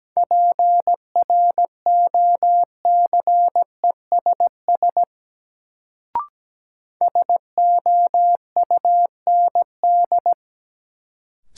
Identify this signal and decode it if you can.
Morse code, heard as a keyed tone, about 17 wpm, 700 Hz. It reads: PROCESS SOUND
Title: Sets of 3 Words - New Word Drill - 17wpm